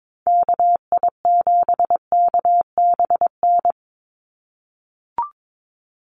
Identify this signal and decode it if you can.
Morse code, heard as a keyed tone, about 22 wpm, 700 Hz. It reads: KI7KBN
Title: USA Call Signs 22wpm